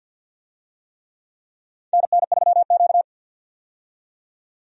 Morse code, heard as a keyed tone, about 51 wpm, 700 Hz. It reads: NN3Q
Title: USA Call Signs - 4 Characters - 50wpm